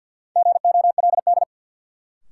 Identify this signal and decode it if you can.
Morse code, heard as a keyed tone, about 52 wpm, 700 Hz. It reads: GOLD
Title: Sentences from Top 1000 Words - Review Part 3 - New Word Drill - 50wpm